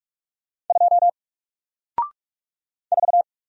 Morse code, heard as a keyed tone, about 44 wpm, 700 Hz. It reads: 2 4
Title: Single Character (Letter:Number:Punctuation:Prosign) 45wpm